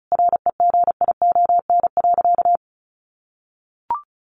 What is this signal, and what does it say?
Morse code, heard as a keyed tone, about 34 wpm, 700 Hz. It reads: REGION.
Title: Sentences from Top 1500 Words 35wpm